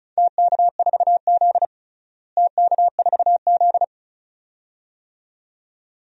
Morse code, heard as a keyed tone, about 34 wpm, 700 Hz. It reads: TK4Z TK4Z
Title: DXCC-100 Call Signs - 4 Characters - Repeated 2x - 35wpm